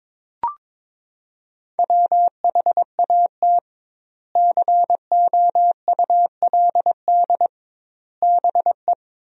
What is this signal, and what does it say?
Morse code, heard as a keyed tone, about 22 wpm, 700 Hz. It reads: WHAT COULD BE
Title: 2x Word-Spacing - Sentences from Top 400 Words - 22wpm